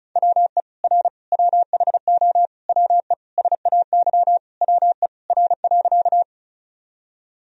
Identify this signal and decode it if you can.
Morse code, heard as a keyed tone, about 34 wpm, 700 Hz. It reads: WE R WHO WE SAY WE R.
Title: Sentences from Top 200 Words and Abbreviations 35wpm